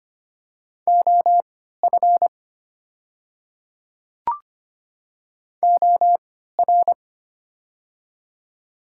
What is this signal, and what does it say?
Morse code, heard as a keyed tone, about 25 wpm, spaced as for 14 wpm, 690 Hz. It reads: OF OR
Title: 3x Character Spacing - Top 500 Words - 25wpm